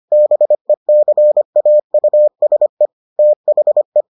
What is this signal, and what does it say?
Morse code, heard as a keyed tone, about 25 wpm, 600 Hz. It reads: BECAUSE THE